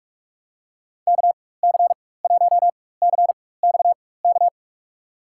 Morse code, heard as a keyed tone, about 47 wpm, spaced as for 21 wpm, 700 Hz. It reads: KC1CXK
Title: USA Call Signs - 6 Characters - 4x Character Spacing - 45wpm